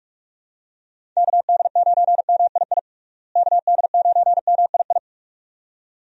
Morse code, heard as a keyed tone, about 46 wpm, 700 Hz. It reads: KD9MII KD9MII
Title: USA Call Signs - 6 Characters - Repeated 2x - 45wpm